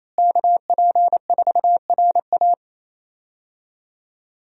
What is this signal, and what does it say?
Morse code, heard as a keyed tone, about 28 wpm, 700 Hz. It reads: KP4RA
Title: DXCC-100 Call Signs - 5 Characters - Repeated 3x - 28wpm